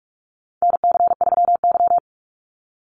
Morse code, heard as a keyed tone, about 44 wpm, 700 Hz. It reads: NC3Y
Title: Call Signs - 45wpm